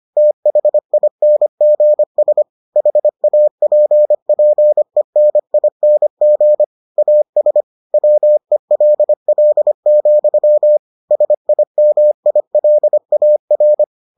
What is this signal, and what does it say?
Morse code, heard as a keyed tone, about 25 wpm, 600 Hz. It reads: THINGS HAPPENING AS WELL, SIMILAR